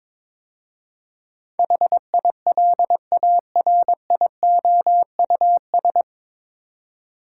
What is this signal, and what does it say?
Morse code, heard as a keyed tone, about 22 wpm, 700 Hz. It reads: HILARIOUS
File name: 9 Letter Words - First 3 Letters Given - 22wpm